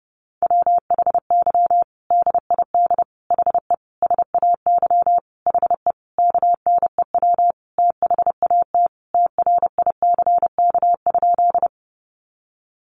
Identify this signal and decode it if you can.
Morse code, heard as a keyed tone, about 30 wpm, 700 Hz. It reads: WHY DID HE SAY HE KNEW THAT TRICK?
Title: Sentences from Top 1500 Words 30wpm